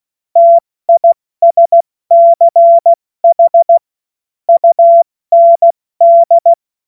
Morse code, heard as a keyed tone, about 16 wpm, 680 Hz. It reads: TISCH UND
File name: Morsekurs / Witze / Witz 11. Mai 2021, 70 F80